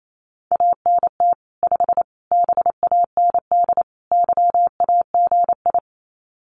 Morse code, heard as a keyed tone, about 28 wpm, 700 Hz. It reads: ANT 5 BAND YAGI
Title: Common QSO Phrases 28wpm